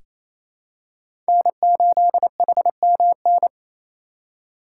Morse code, heard as a keyed tone, about 28 wpm, 700 Hz. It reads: N8HMN
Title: USA Call Signs - 8x Character Spacing - 28wpm